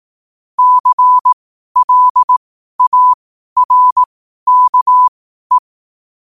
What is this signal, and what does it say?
Morse code, heard as a keyed tone, about 18 wpm, spaced as for 12 wpm, 1000 Hz. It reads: CLARKE
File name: qotdfiles / 12wpm